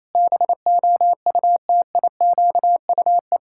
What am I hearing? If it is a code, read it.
Morse code, heard as a keyed tone, about 27 wpm, 700 Hz. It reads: BOUTIQUE
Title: Pangram Challenge v2 - 28wpm